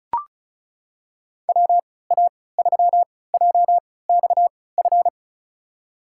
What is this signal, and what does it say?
Morse code, heard as a keyed tone, about 36 wpm, spaced as for 20 wpm, 700 Hz. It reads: WA3JXF
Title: USA Call Signs - 3x Character Spacing - 35wpm